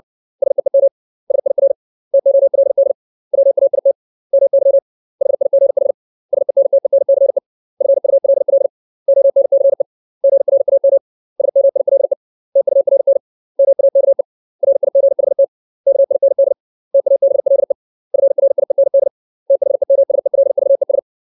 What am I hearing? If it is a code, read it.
Morse code, heard as a keyed tone, about 62 wpm, 550 Hz. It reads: SEEM SIR TOLD WAIT MY HIGH SENTENCE FULL ONCE GRAM SMILE TURN MAKE RIGHT KIND TABLE FRIEND THEMSELVES
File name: 632 words / 632 Words 60WPM